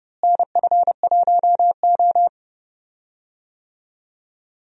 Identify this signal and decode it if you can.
Morse code, heard as a keyed tone, about 30 wpm, 700 Hz. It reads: NF1O
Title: USA Call Signs - 4 Characters - Repeated 2x - 30wpm